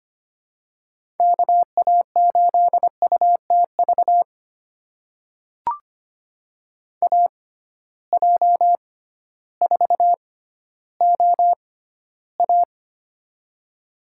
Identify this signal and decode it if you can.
Morse code, heard as a keyed tone, about 25 wpm, spaced as for 14 wpm, 700 Hz. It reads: KA8UTV AJ4OA
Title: USA Call Signs - 6x Character Spacing - 25wpm